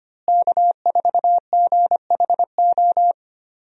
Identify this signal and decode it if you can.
Morse code, heard as a keyed tone, about 25 wpm, 700 Hz. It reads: K4GHO
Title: USA Call Signs - 5 Characters - Repeated 2x - 25wpm